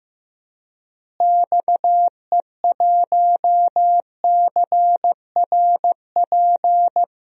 Morse code, heard as a keyed tone, about 15 wpm, 700 Hz. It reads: XE1CRP
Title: DXCC-100 Call Signs - 6 Characters - Repeated 3x - 15wpm